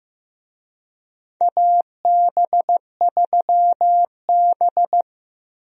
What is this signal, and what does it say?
Morse code, heard as a keyed tone, about 16 wpm, 700 Hz. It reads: AB3B
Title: USA Call Signs - 4 Characters - Repeated 3x - 15wpm